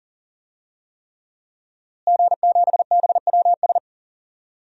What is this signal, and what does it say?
Morse code, heard as a keyed tone, about 40 wpm, 700 Hz. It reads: G7BWS
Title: DXCC-100 Call Signs - 5 Characters - Repeated 3x - 40wpm